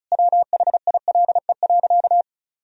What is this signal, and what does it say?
Morse code, heard as a keyed tone, about 36 wpm, 700 Hz. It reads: WHILE.
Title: Sentences from Top 500 Words and Abbreviations 35wpm